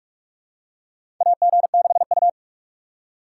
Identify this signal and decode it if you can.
Morse code, heard as a keyed tone, about 42 wpm, 700 Hz. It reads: AG6U
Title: USA Call Signs - 2x Character Spacing - 45wpm